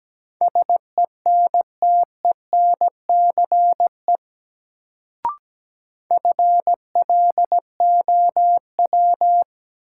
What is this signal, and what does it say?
Morse code, heard as a keyed tone, about 17 wpm, 700 Hz. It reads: SENTENCE FLOW
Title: Sentences from Top 1200 Words - New Word Drill Alt - 17wpm